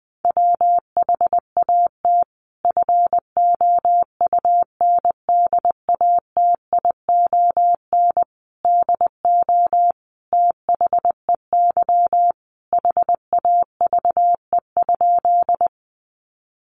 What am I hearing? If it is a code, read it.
Morse code, heard as a keyed tone, about 20 wpm, 700 Hz. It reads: WHAT FOUNDATION DO THEY HAVE?
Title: Sentences from Top 1200 Words 20wpm